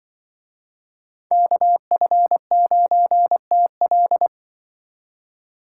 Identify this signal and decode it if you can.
Morse code, heard as a keyed tone, about 24 wpm, 700 Hz. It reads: KF9TL
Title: Speed Racing - USA 2x2 Call Signs 20wpm